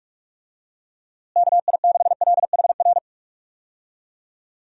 Morse code, heard as a keyed tone, about 41 wpm, 700 Hz. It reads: KI6LHR
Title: USA Call Signs - 6 Characters - Repeated 3x - 45wpm